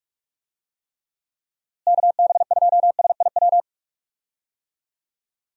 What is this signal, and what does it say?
Morse code, heard as a keyed tone, about 46 wpm, 700 Hz. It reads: KB2SIW
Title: USA Call Signs - 6 Characters - Repeated 3x - 45wpm